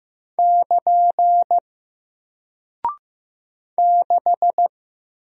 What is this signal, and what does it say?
Morse code, heard as a keyed tone, about 15 wpm, 700 Hz. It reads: <KN> 6
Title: Single Character (Letter:Number:Punctuation:Prosign) Alt 15wpm